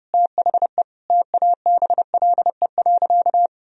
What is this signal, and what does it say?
Morse code, heard as a keyed tone, about 32 wpm, 700 Hz. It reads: THE TABLE.
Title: Sentences from Top 1000 Words - Review Part 2 - 30wpm